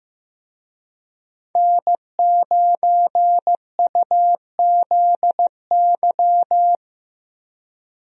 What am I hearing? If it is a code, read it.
Morse code, heard as a keyed tone, about 15 wpm, 700 Hz. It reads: N9UZY
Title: USA Call Signs - 5 Characters - Repeated 3x - 15wpm